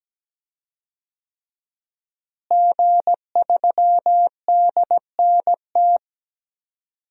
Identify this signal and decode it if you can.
Morse code, heard as a keyed tone, about 17 wpm, 700 Hz. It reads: G3DNT
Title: DXCC-100 Call Signs - 5 Characters - Repeated 3x - 17wpm